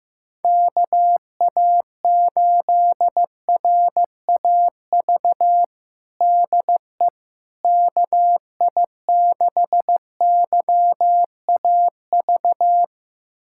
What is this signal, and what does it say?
Morse code, heard as a keyed tone, about 15 wpm, 700 Hz. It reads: KA8RAV DE KI6YAV
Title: USA Callsign DE Callsign - 6 Characters - 15wpm